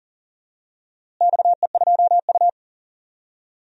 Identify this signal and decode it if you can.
Morse code, heard as a keyed tone, about 40 wpm, 700 Hz. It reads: XE2U
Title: DXCC-100 Call Signs - 4 Characters - Repeated 3x - 40wpm